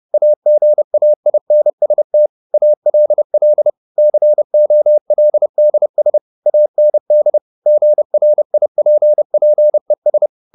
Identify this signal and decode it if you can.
Morse code, heard as a keyed tone, about 30 wpm, 600 Hz. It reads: AGAINST ALL COLDS AND GRIPPES
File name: the old man and the sea / the old man and the sea 1 30wpm